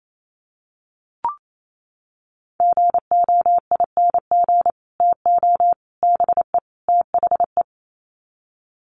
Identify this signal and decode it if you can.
Morse code, heard as a keyed tone, about 28 wpm, 700 Hz. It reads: GOING TO BE THE
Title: Sets of 4 Words 28wpm